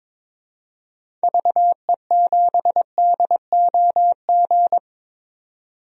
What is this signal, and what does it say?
Morse code, heard as a keyed tone, about 22 wpm, 700 Hz. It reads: VE7DOG
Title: DXCC-100 Call Signs - 6 Characters - 22wpm